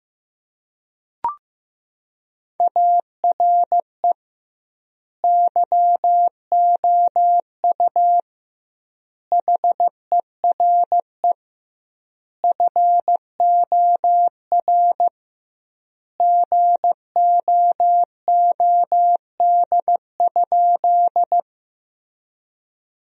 Morse code, heard as a keyed tone, about 15 wpm, 700 Hz. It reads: ARE YOU HERE FOR GOOD?
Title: 2x Word-Spacing - Sentences from Top 200 Words - 15wpm